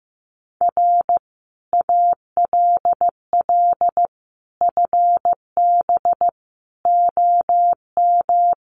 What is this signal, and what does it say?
Morse code, heard as a keyed tone, about 15 wpm, 700 Hz. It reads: R ALL FB OM
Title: Common QSO Phrases 15wpm